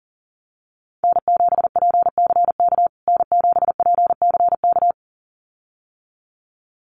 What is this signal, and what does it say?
Morse code, heard as a keyed tone, about 40 wpm, 700 Hz. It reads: N7PCK N7PCK
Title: USA Call Signs Repeated 2x 40wpm